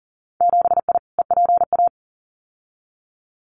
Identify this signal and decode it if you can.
Morse code, heard as a keyed tone, about 38 wpm, 700 Hz. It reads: 7I EPA
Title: Winter Field Day Exchanges Encore 40wpm